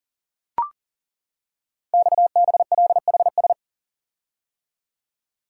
Morse code, heard as a keyed tone, about 39 wpm, 700 Hz. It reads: XBLHS
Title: Code Groups - 5 Letters - 40wpm